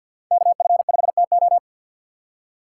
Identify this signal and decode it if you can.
Morse code, heard as a keyed tone, about 52 wpm, 700 Hz. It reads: KF5TO
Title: USA Call Signs - 5 Characters - 2x Character Spacing - 50wpm